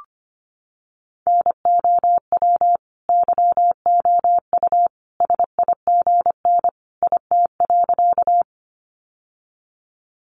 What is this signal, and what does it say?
Morse code, heard as a keyed tone, about 25 wpm, 700 Hz. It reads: NOW YOU SIGN IT.
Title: Sentences from Top 1000 Words 25wpm